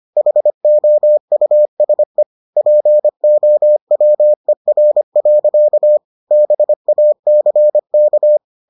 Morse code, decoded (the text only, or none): HOUSE POWER. BACK